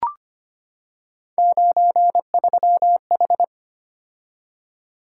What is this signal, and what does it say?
Morse code, heard as a keyed tone, about 25 wpm, 700 Hz. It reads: 93H
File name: Three Letter-Number Combinations - 25wpm